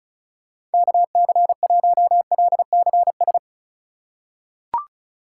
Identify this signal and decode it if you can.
Morse code, heard as a keyed tone, about 36 wpm, 700 Hz. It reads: KC1LCS
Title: USA Call Signs - 6 Characters - 3x Character Spacing - 35wpm